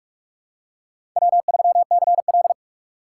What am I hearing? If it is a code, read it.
Morse code, heard as a keyed tone, about 47 wpm, 700 Hz. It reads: W3CL
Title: USA Call Signs - 4 Characters - 4x Character Spacing - 45wpm